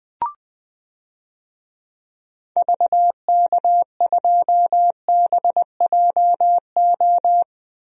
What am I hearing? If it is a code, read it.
Morse code, heard as a keyed tone, about 20 wpm, 700 Hz. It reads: VK2BJO
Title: DXCC-100 Call Signs - 6 Characters - Repeated 2x - 20wpm